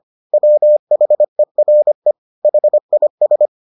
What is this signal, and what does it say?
Morse code, heard as a keyed tone, about 25 wpm, 600 Hz. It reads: WHERE HIS